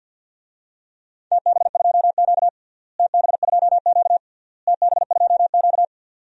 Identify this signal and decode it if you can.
Morse code, heard as a keyed tone, about 51 wpm, 700 Hz. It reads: TB2Q TB2Q TB2Q
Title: DXCC-100 Call Signs - 4 Characters - Repeated 3x - 50wpm